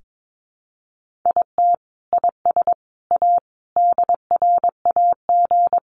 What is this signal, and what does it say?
Morse code, heard as a keyed tone, about 22 wpm, 700 Hz. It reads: IT IS A DRAG
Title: Sentences from Top 1500 Words 22wpm